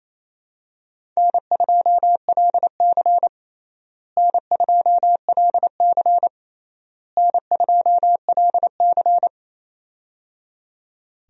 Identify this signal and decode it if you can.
Morse code, heard as a keyed tone, about 27 wpm, 700 Hz. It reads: N2LC N2LC N2LC
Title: USA Call Signs - 4 Characters - Repeated 3x - 28wpm